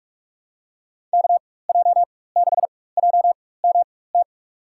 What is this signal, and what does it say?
Morse code, heard as a keyed tone, about 46 wpm, spaced as for 21 wpm, 700 Hz. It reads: KJ6JMT
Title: USA Call Signs - 6 Characters - 4x Character Spacing - 45wpm